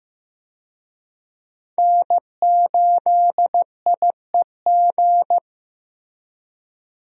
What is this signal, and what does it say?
Morse code, heard as a keyed tone, about 16 wpm, 700 Hz. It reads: N8IEG
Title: USA Call Signs - 5 Characters - Repeated 2x - 15wpm